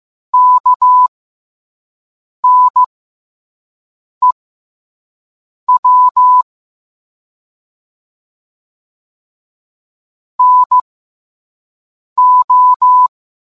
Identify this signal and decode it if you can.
Morse code, heard as a keyed tone, about 15 wpm, spaced as for 5 wpm, 1000 Hz. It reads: KNEW NO